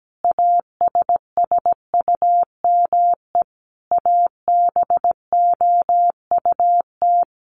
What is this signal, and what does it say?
Morse code, heard as a keyed tone, about 17 wpm, 700 Hz. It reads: ASSUME ABOUT
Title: Sentences from Top 1500 Words 17wpm